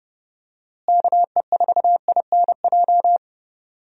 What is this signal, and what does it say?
Morse code, heard as a keyed tone, about 30 wpm, 700 Hz. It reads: KE4INJ
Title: USA Call Signs - 6 Characters - Repeated 3x - 30wpm